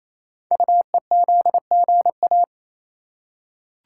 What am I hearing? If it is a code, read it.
Morse code, heard as a keyed tone, about 27 wpm, 700 Hz. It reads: UEZGA
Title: Code Groups - 5 Letter:Numbers - 28wpm